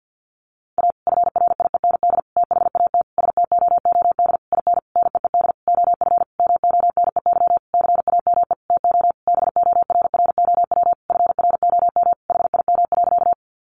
Morse code, heard as a keyed tone, about 48 wpm, 700 Hz. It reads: A FRIEND THAT STOOD IN NEED OF MONEY CAME TO BORROW FROM HIM.
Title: Sentences from Top 1500 Words 50wpm